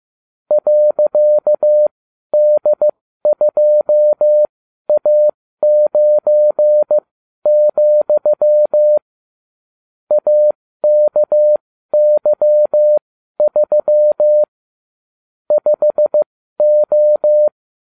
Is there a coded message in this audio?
Yes: Morse code from 0.5 to 17.5 s, heard as a keyed tone, about 15 wpm, spaced as for 12 wpm, 600 Hz. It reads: .D2A9, AKY3 5O